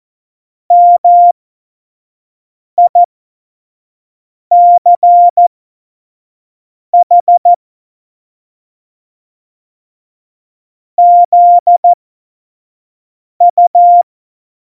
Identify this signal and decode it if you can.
Morse code, heard as a keyed tone, about 14 wpm, spaced as for 5 wpm, 700 Hz. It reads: MICH ZU
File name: Morsekurs / Witze / Witz 06 March 2018, 25 F70